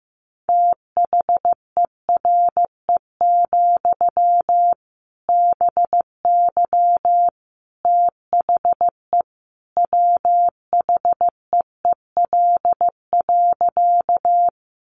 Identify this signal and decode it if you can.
Morse code, heard as a keyed tone, about 15 wpm, 700 Hz. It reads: THERE, BY THE WHEEL.
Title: Sentences from Top 500 Words 15wpm